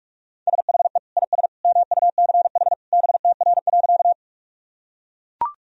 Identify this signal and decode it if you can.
Morse code, heard as a keyed tone, about 41 wpm, 700 Hz. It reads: SHE IS MUCH BTR.
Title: Sentences from Top 400 Words and Abbreviations 45wpm